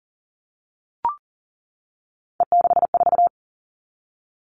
Morse code, heard as a keyed tone, about 40 wpm, 700 Hz. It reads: E64
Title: All Cut Numbers - 3-Digit 40wpm